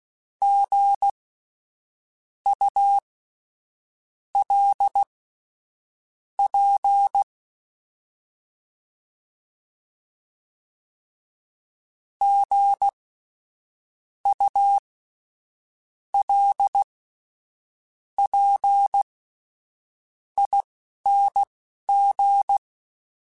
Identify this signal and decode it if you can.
Morse code, heard as a keyed tone, about 16 wpm, spaced as for 6 wpm, 800 Hz. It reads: GULP GULPING
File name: Spaced Alphas / Spaced Alphas mp3s / Gs sp @10